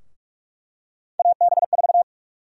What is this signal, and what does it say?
Morse code, heard as a keyed tone, about 46 wpm, 700 Hz. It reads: AB4
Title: USA Call Sign Prefixes - Repeated 2x - 45wpm